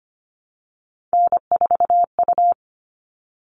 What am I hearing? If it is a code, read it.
Morse code, heard as a keyed tone, about 25 wpm, 700 Hz. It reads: N4U
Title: All Cut Numbers - 3-Digit 25wpm